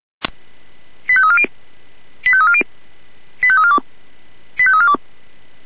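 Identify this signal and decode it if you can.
Five-tone selective call (ZVEI), heard as quick runs of five tones, tones of about 70 ms. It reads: 85340 85340 85342 85342
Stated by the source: ZVEI five-tone selective call received by SDR.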